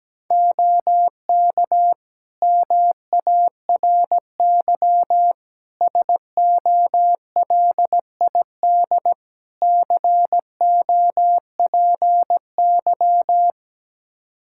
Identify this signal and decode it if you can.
Morse code, heard as a keyed tone, about 17 wpm, 700 Hz. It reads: OK MARY SOLID COPY
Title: Practice QSO - Exchange 3 - More Info - 17wpm